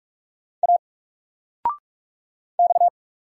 Morse code, heard as a keyed tone, about 45 wpm, 700 Hz. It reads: A X
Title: Single Letter-Number Alt 45wpm